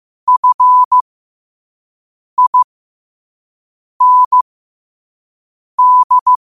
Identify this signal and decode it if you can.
Morse code, heard as a keyed tone, about 15 wpm, spaced as for 5 wpm, 1000 Hz. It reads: FIND
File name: qotdfiles / 5wpm